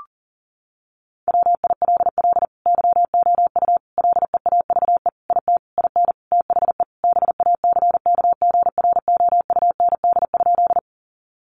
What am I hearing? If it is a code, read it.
Morse code, heard as a keyed tone, about 40 wpm, 700 Hz. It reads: WILL YOU LEAVE IT IN THE BACKGROUND?
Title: Sentences from Top 1300 Words 40wpm